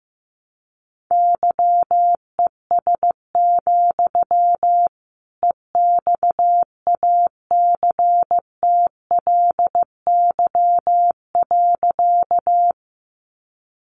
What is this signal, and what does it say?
Morse code, heard as a keyed tone, about 15 wpm, 700 Hz. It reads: YES, EXACTLY.
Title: Sentences from Top 900 Words 15wpm